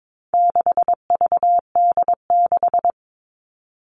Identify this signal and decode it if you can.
Morse code, heard as a keyed tone, about 22 wpm, 700 Hz. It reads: 6VD6
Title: All Cut Numbers - 4-Digit 22wpm